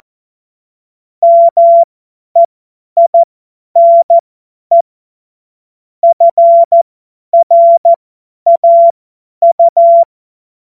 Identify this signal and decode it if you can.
Morse code, heard as a keyed tone, about 14 wpm, spaced as for 10 wpm, 680 Hz. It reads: MEINE FRAU
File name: Morsekurs / Witze / Witz 22. Februar 2022 50 F70